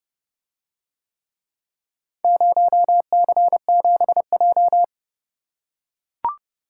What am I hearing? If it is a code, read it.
Morse code, heard as a keyed tone, about 30 wpm, 700 Hz. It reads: 0C7J